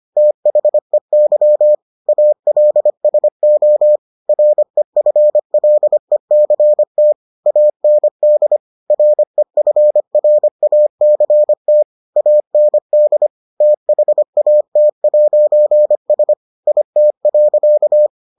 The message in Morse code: THEY ALSO REFLECT AND REFRACT AND THAT'S IT.

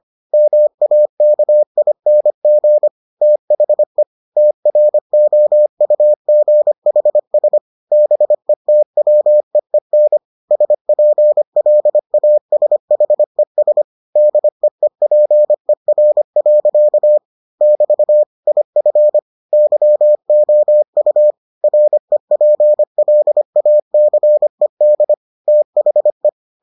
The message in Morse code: MAKING THE TROUGHS BETWEEN SPLASHES DEEPER. <BT> IF YOU REPLACED THE

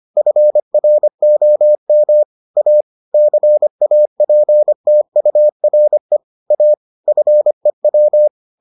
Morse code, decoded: FROM A CAPTURE A FEW

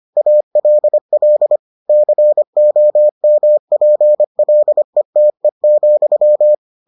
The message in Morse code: ALL COMPLETE,

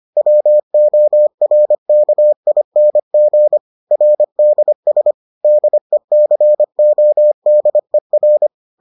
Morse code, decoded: WORKING RDS DECODER

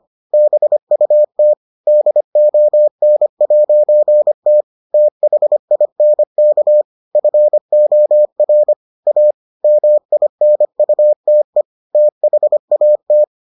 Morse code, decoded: BUT DON'T THINK FOR A MINUTE THAT

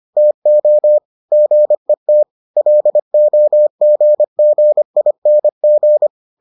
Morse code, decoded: TO GET LOGGING

